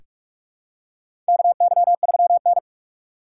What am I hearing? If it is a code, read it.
Morse code, heard as a keyed tone, about 45 wpm, 700 Hz. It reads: KY3N
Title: USA Call Signs - 4 Characters - 3x Character Spacing - 45wpm